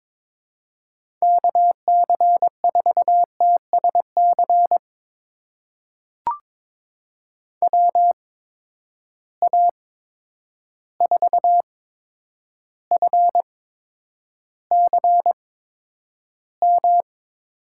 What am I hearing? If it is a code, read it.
Morse code, heard as a keyed tone, about 22 wpm, spaced as for 9 wpm, 700 Hz. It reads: KC4TSC WA4FCM
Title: USA Call Signs - 6 Characters - 8x Character Spacing - 22wpm